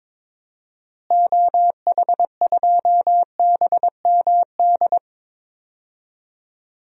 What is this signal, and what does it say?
Morse code, heard as a keyed tone, about 22 wpm, 700 Hz. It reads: OH2BMD
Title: DXCC-100 Call Signs - 6 Characters - Repeated 3x - 22wpm